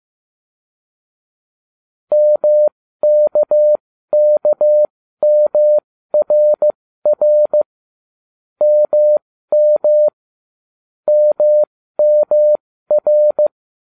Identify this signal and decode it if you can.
Morse code, heard as a keyed tone, about 15 wpm, spaced as for 13 wpm, 600 Hz. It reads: MKKMRR MM MMR